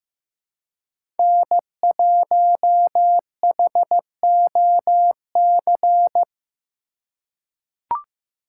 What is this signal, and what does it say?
Morse code, heard as a keyed tone, about 16 wpm, 700 Hz. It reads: N1HOC
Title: USA Call Signs - 5 Characters - 3x Character Spacing - 15wpm